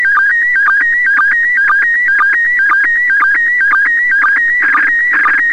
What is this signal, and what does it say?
SSTV (slow-scan television), PD 120.
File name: ISS 23h00 UTC 145MHz800 pic04